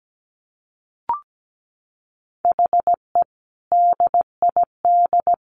Morse code, heard as a keyed tone, about 17 wpm, 700 Hz. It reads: HE DID
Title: Sentences from Top 1500 Words 17wpm